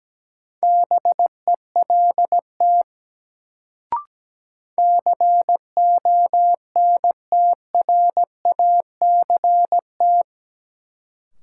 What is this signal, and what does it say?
Morse code, heard as a keyed tone, about 17 wpm, 700 Hz. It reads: BELT CONTRACT
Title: Sentences from Top 1400 Words - New Word Drill Alt - 17wpm